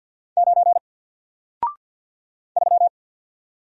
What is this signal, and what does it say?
Morse code, heard as a keyed tone, about 50 wpm, 700 Hz. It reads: 9 3
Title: Single Letter-Number 50wpm